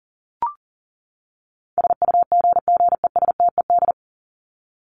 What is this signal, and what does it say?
Morse code, heard as a keyed tone, about 38 wpm, 700 Hz. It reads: SUGGESTED
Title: ED Suffix - Top 100 Words with Base Words 40wpm